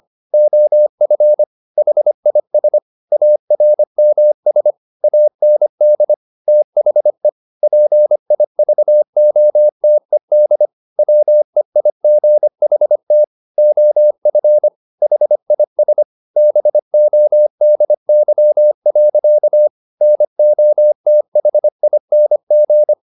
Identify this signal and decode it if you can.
Morse code, heard as a keyed tone, about 25 wpm, 600 Hz. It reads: OF HIS ARMS AND THE PIVOTED WEIGHT OF HIS BODY. NOTHING
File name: the old man and the sea part 1 25wpm